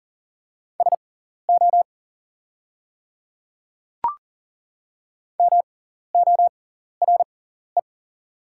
Morse code, heard as a keyed tone, about 40 wpm, spaced as for 14 wpm, 690 Hz. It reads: SO MORE
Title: 6x Character Spacing - Top 100 Words - 40wpm